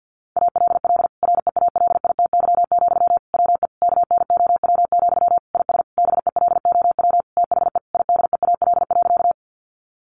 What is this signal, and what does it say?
Morse code, heard as a keyed tone, about 50 wpm, 700 Hz. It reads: ALL REALITY, WE KNOW, IS BELOW THE IDEAL.
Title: Sentences from Top 1400 Words 50wpm